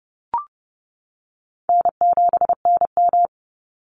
Morse code, heard as a keyed tone, about 30 wpm, 700 Hz. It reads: N7NM
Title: Easy Call Signs - 30wpm